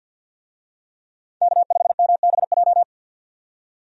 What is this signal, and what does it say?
Morse code, heard as a keyed tone, about 50 wpm, 700 Hz. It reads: K5MBJ
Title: USA Call Signs - 5 Characters - 50wpm